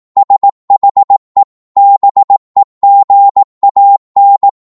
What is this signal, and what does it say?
Morse code, heard as a keyed tone, about 18 wpm, 800 Hz. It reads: SHE BEGAN
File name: Morse / psg-18wpm